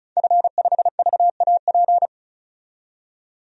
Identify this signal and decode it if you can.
Morse code, heard as a keyed tone, about 36 wpm, 700 Hz. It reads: F5VAP
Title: DXCC-100 Call Signs - 5 Characters - 35wpm